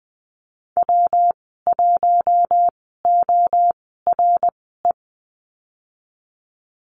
Morse code, heard as a keyed tone, about 20 wpm, spaced as for 15 wpm, 700 Hz. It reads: W1ORE
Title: 173 Lesson 28g - Number 9 - Call Signs 20wpm